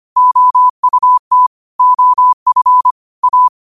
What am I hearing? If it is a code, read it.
Morse code, heard as a keyed tone, about 25 wpm, 1000 Hz. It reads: OUT OF A
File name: qotdfiles / 25wpm